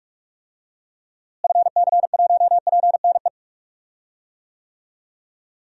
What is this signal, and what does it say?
Morse code, heard as a keyed tone, about 47 wpm, 700 Hz. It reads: FC1PNE
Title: DXCC-100 Call Signs - 6 Characters - Repeated 2x - 45wpm